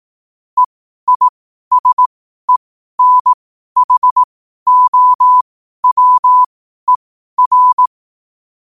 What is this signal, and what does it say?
Morse code, heard as a keyed tone, about 18 wpm, spaced as for 12 wpm, 1000 Hz. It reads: EISENHOWER
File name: qotdfiles / 12wpm